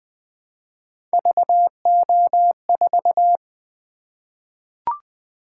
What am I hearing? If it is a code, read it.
Morse code, heard as a keyed tone, about 20 wpm, 700 Hz. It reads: VO4
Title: Three Letter-Number Combinations - 20wpm